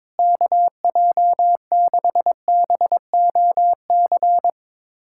Morse code, heard as a keyed tone, about 22 wpm, 700 Hz. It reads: KJ6BOC
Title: USA Callsign DE Callsign - 6 Characters - 22wpm